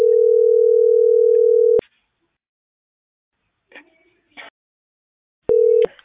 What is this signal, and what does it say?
Ringback tone (North America), 440 + 480 Hz.